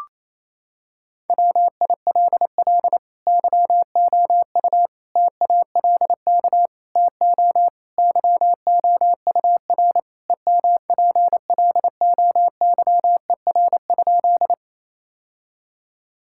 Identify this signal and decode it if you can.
Morse code, heard as a keyed tone, about 28 wpm, 700 Hz. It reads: WILL YOU TALK TO YOUR EMPLOYER?
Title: Sentences from Top 1600 Words 28wpm